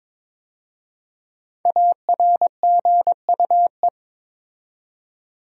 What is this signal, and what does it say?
Morse code, heard as a keyed tone, about 22 wpm, 700 Hz. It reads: ARGUE
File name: 5 Letter Words - First 3 Letters Given - 22wpm